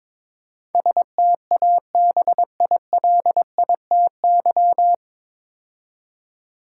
Morse code, heard as a keyed tone, about 22 wpm, 700 Hz. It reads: STABILITY
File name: 9 Letter Words - First 3 Letters Given - 22wpm